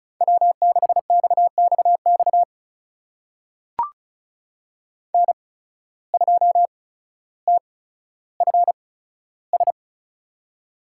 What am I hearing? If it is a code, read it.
Morse code, heard as a keyed tone, about 36 wpm, spaced as for 15 wpm, 700 Hz. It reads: W6XXX N2TFS
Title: USA Call Signs - 5 Characters - 8x Character Spacing - 35wpm